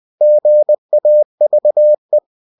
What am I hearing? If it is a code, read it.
Morse code, heard as a keyed tone, about 20 wpm, 600 Hz. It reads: GAVE
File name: cw-books / winnie-the-pooh / winnie-20wpm-page-0058